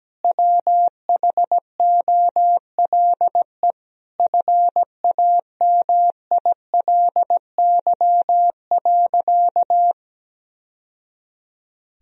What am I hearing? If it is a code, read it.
Morse code, heard as a keyed tone, about 17 wpm, 700 Hz. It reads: WHOLE FAMILY.
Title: Sentences from Top 1000 Words - Review Part 4 - 17wpm